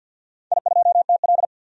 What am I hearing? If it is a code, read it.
Morse code, heard as a keyed tone, about 51 wpm, 700 Hz. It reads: I2TL
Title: DXCC-100 Call Signs - 4 Characters - 50wpm